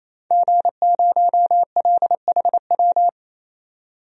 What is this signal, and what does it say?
Morse code, heard as a keyed tone, about 27 wpm, 700 Hz. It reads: G0LHW
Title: DXCC-100 Call Signs - 5 Characters - 28wpm